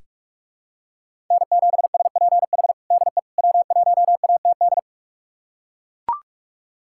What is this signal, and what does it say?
Morse code, heard as a keyed tone, about 42 wpm, 700 Hz. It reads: N7SPH DE W1ATD
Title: USA Callsign DE Callsign - 5 Characters - 45wpm